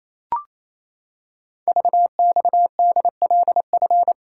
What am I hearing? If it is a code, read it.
Morse code, heard as a keyed tone, about 29 wpm, 700 Hz. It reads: VXDLF
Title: Code Groups - 5 Letters - 28wpm